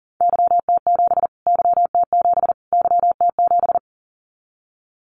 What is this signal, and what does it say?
Morse code, heard as a keyed tone, about 40 wpm, 700 Hz. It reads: YT7 YT7 YT7
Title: Callsign Prefixes for Top 25 DXCC - Repeated 3x - 40wpm